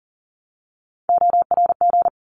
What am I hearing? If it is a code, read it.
Morse code, heard as a keyed tone, about 40 wpm, 700 Hz. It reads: ORG
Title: ARRL Sections 40wpm